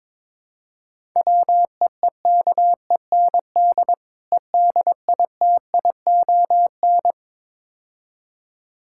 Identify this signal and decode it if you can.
Morse code, heard as a keyed tone, about 21 wpm, 700 Hz. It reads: WEEKEND EDITION
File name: practice / 22wpm / Build A Phrase - 5-Word Phrases - 22wpm